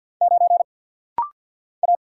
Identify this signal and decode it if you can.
Morse code, heard as a keyed tone, about 49 wpm, 700 Hz. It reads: 9 A
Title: Single Letter-Number Alt - Rapid-Fire 50wpm